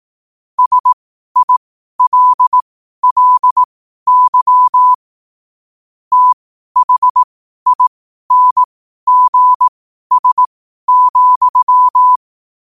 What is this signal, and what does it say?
Morse code, heard as a keyed tone, about 18 wpm, spaced as for 12 wpm, 1000 Hz. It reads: SILLY THINGS,
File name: qotdfiles / 12wpm